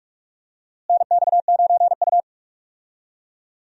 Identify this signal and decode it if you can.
Morse code, heard as a keyed tone, about 44 wpm, 700 Hz. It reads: NX9U
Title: USA Call Signs - 4 Characters - Repeated 3x - 45wpm